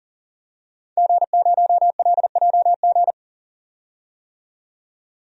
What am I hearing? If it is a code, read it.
Morse code, heard as a keyed tone, about 40 wpm, 700 Hz. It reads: G0LJG
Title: DXCC-100 Call Signs - 5 Characters - 40wpm